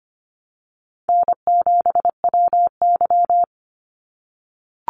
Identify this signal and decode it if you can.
Morse code, heard as a keyed tone, about 25 wpm, 700 Hz. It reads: N7WY
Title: Easy Call Signs - 25wpm